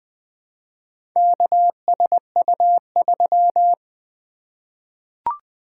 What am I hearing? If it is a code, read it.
Morse code, heard as a keyed tone, about 20 wpm, 700 Hz. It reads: KSU3
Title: Four Letter-Number Combinations - 20wpm